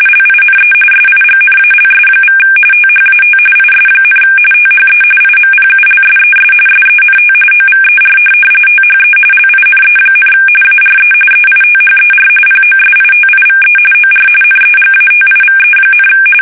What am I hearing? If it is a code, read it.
RTTY, about 45.45 baud. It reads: LINE SIGNALS= THAT IS, THE IMPULSES SENT OVER THE LINE, ARE EITHER POSITIVE OR NEGATIVE,
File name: fsk850